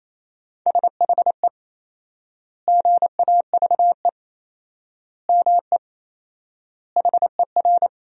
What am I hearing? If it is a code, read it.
Morse code, heard as a keyed tone, about 28 wpm, 700 Hz. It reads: SHE GAVE ME HER
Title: 4x Word-Spacing - Sentences from Top 500 Words - 28wpm